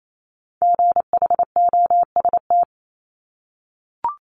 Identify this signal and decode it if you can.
Morse code, heard as a keyed tone, about 28 wpm, 700 Hz. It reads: GHOST
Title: Multi-letter Phonemes Repeated 3x 28wpm